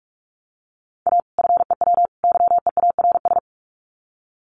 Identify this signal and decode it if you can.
Morse code, heard as a keyed tone, about 44 wpm, 700 Hz. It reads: A FEW YEARS
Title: Sets of 3 Words 45wpm